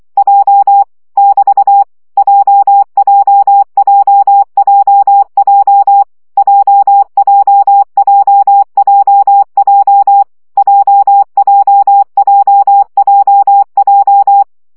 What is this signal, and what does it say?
Morse code, heard as a keyed tone, about 24 wpm, 800 Hz. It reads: J <BT> JJJJJ JJJJJ JJJJJ
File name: CW gyakorlas j 20 baud